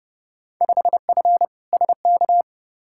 Morse code, heard as a keyed tone, about 30 wpm, 700 Hz. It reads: 5F SK
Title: Field Day Exchanges Encore - 30wpm